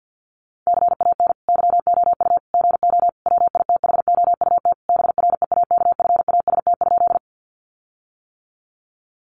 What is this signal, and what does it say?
Morse code, heard as a keyed tone, about 50 wpm, 700 Hz. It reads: CAN YOU GO WITHOUT BREAKFAST?
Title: Sentences from Top 1500 Words 50wpm